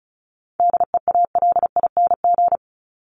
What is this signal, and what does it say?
Morse code, heard as a keyed tone, about 34 wpm, 700 Hz. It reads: DEALING
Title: ING Suffix - Top 100 Words with Base Words 35wpm